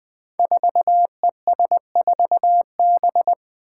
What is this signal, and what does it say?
Morse code, heard as a keyed tone, about 20 wpm, 700 Hz. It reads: 4ES4B
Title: Five Challenging Letter-Number Combinations 20wpm